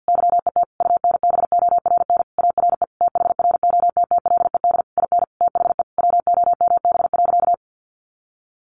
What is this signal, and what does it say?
Morse code, heard as a keyed tone, about 50 wpm, 700 Hz. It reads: YET UNBORN ARE THROTTLED IN THE WOMB.
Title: A Sound Of Thunder 50wpm